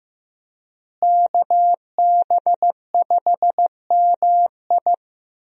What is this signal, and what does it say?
Morse code, heard as a keyed tone, about 15 wpm, 700 Hz. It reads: KB5MI
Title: USA Call Signs - 5 Characters - Repeated 3x - 15wpm